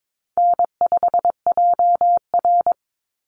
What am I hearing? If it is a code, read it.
Morse code, heard as a keyed tone, about 22 wpm, 700 Hz. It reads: N5JR
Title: Easy Call Signs Repeated 3x - 22wpm